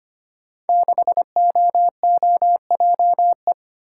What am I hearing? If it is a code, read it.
Morse code, heard as a keyed tone, about 25 wpm, 700 Hz. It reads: 6OOJE
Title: Code Groups - 5 Letter:Numbers - 25wpm